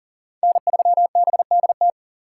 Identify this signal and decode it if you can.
Morse code, heard as a keyed tone, about 40 wpm, 700 Hz. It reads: N3BDT
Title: USA Call Signs - 5 Characters - 3x Character Spacing - 40wpm